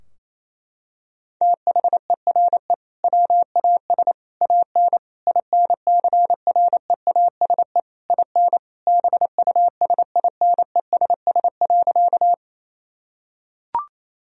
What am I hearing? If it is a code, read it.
Morse code, heard as a keyed tone, about 28 wpm, 700 Hz. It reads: THERE WAS AN INCREASE IN BUSINESS.
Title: Sentences from Top 1000 Words - Review Part 4 - 28wpm